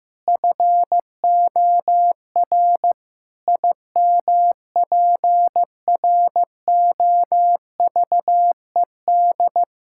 Morse code, heard as a keyed tone, about 16 wpm, 700 Hz. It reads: FOR IMPROVED
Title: Pangram Challenge v2 - 15wpm